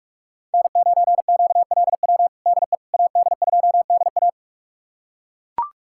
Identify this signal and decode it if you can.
Morse code, heard as a keyed tone, about 41 wpm, 700 Hz. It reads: N9QLW DE AD2DA
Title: USA Callsign DE Callsign - 5 Characters - 45wpm